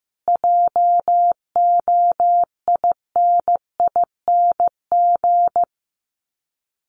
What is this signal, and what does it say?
Morse code, heard as a keyed tone, about 14 wpm, 700 Hz. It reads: JOINING
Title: Sets of 5 Words Encore 15wpm